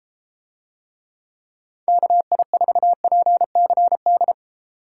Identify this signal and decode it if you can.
Morse code, heard as a keyed tone, about 33 wpm, 700 Hz. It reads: KI4PCD